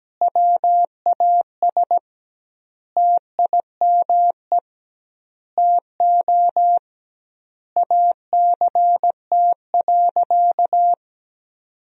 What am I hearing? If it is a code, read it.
Morse code, heard as a keyed tone, about 17 wpm, 700 Hz. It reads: WAS TIME TO ACT.
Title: 2x Word-Spacing - Sentences from Top 200 Words - 17wpm